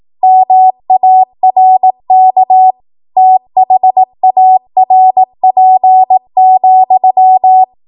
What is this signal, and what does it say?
Morse code, heard as a keyed tone, about 18 wpm, 750 Hz. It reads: MARK THARP,